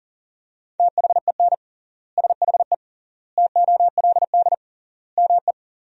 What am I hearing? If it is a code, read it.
Morse code, heard as a keyed tone, about 40 wpm, 700 Hz. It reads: THEN SHE TOLD ME
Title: 3x Word-Spacing - Sentences from Top 400 Words - 40wpm